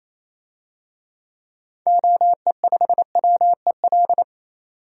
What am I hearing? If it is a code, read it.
Morse code, heard as a keyed tone, about 28 wpm, 700 Hz. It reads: OE5WEL
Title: DXCC-100 Call Signs - 6 Characters - 28wpm